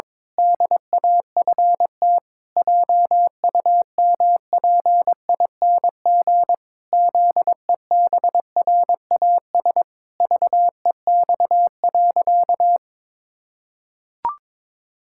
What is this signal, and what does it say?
Morse code, heard as a keyed tone, about 21 wpm, 700 Hz. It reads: DAFT JUMPING ZEBRAS VEX.
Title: Pangram Challenge v2 - 22wpm